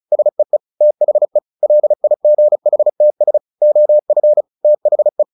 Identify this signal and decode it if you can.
Morse code, heard as a keyed tone, about 34 wpm, 600 Hz. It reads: SEE THE LIGHTS OF THE
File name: the old man and the sea 2 35wpm